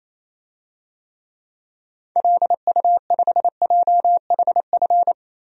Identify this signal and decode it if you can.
Morse code, heard as a keyed tone, about 28 wpm, 700 Hz. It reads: LU5JHF
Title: DXCC-100 Call Signs - 6 Characters - 28wpm